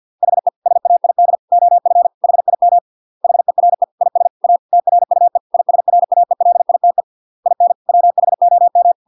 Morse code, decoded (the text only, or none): HE SAID OF HIM HERE IS A TRUE ISRAELITE IN WHOM